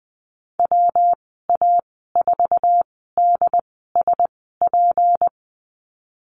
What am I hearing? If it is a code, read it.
Morse code, heard as a keyed tone, about 19 wpm, spaced as for 14 wpm, 700 Hz. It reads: WA4DSP
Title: 154 Lesson 25g - Letter p - Call Signs 20wpm